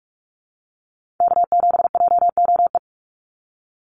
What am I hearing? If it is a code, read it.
Morse code, heard as a keyed tone, about 46 wpm, 700 Hz. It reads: K7JOE
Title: Call Signs - 45wpm